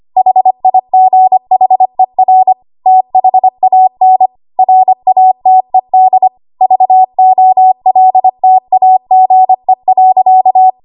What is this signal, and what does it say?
Morse code, heard as a keyed tone, about 25 wpm, 750 Hz. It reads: HIGHER THAN RATED VOLTAGE.
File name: Archive / 25 WPM / 25WPM